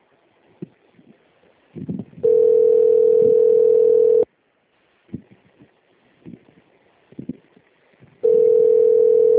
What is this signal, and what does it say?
Ringback tone (North America), 440 + 480 Hz.
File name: out- - - - - .6007